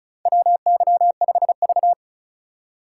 Morse code, heard as a keyed tone, about 34 wpm, 700 Hz. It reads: WY5V
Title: USA Call Signs - 4 Characters - 2x Character Spacing - 35wpm